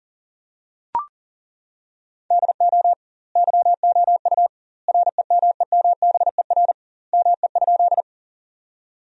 Morse code, heard as a keyed tone, about 40 wpm, 700 Hz. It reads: DO YOU REMEMBER ME?
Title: 2x Word-Spacing - Sentences from Top 400 Words - 40wpm